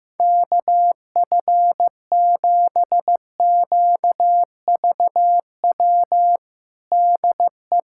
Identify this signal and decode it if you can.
Morse code, heard as a keyed tone, about 16 wpm, 700 Hz. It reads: KF7QVW DE
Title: USA Callsign DE Callsign 15wpm